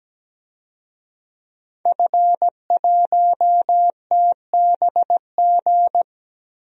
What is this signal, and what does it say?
Morse code, heard as a keyed tone, about 17 wpm, 700 Hz. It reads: F1TBG
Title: DXCC-100 Call Signs - 5 Characters - Repeated 3x - 17wpm